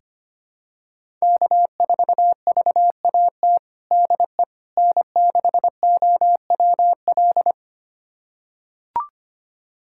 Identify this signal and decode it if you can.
Morse code, heard as a keyed tone, about 25 wpm, 700 Hz. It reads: K4VAT DE N6OWL
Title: USA Callsign DE Callsign - 5 Characters - 25wpm